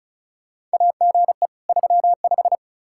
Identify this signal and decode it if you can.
Morse code, heard as a keyed tone, about 36 wpm, 700 Hz. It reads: AGE 35
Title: Practice QSO - Exchange 3 - More Info - 35wpm